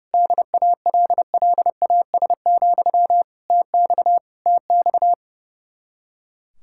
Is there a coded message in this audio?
Yes: Morse code from 0.1 to 5.1 s, heard as a keyed tone, about 38 wpm, 700 Hz. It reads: DALLAS, TX TX